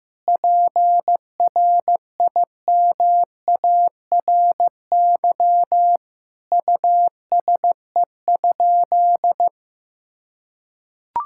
Morse code, heard as a keyed tone, about 15 wpm, 700 Hz. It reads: PRIMARY USE?
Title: Sentences from Top 1000 Words - Review Part 5 - 15wpm